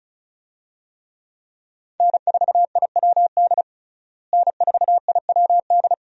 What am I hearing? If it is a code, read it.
Morse code, heard as a keyed tone, about 34 wpm, 700 Hz. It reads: N4IWD N4IWD
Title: USA Call Signs - 5 Characters - Repeated 3x - 35wpm